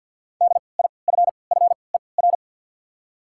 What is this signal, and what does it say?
Morse code, heard as a keyed tone, about 53 wpm, 700 Hz. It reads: D I F F E R
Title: 3x Character Spacing - Top 200 Words - 50wpm